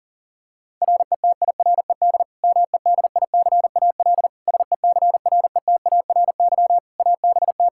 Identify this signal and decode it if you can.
Morse code, heard as a keyed tone, about 40 wpm, 700 Hz. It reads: RETIRED MEDICAL SECRETARY ABT
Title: Practice QSO - Exchange 3 - More Info - 40wpm